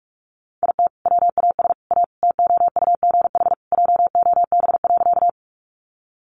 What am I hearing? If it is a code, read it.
Morse code, heard as a keyed tone, about 44 wpm, 700 Hz. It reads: IT WAS A TOUGH JOB.
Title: Sentences from Top 1200 Words 45wpm